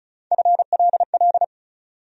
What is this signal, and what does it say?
Morse code, heard as a keyed tone, about 34 wpm, 700 Hz. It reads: FLL
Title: USA Call Sign Postfixes - 3 Letters 35wpm